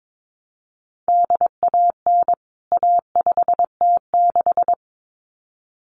Morse code, heard as a keyed tone, about 22 wpm, 700 Hz. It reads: DAN A5T6
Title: CWT Exchange 22wpm